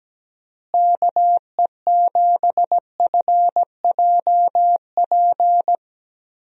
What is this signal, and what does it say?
Morse code, heard as a keyed tone, about 17 wpm, 700 Hz. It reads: KE7FJP
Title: USA Call Signs - 6 Characters - Repeated 3x - 17wpm